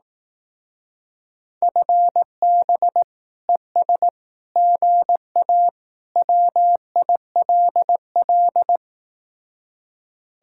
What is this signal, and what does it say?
Morse code, heard as a keyed tone, about 18 wpm, 700 Hz. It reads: FB ES GA WILL
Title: Speed-Racing Common QSO Phrases Encore - 15wpm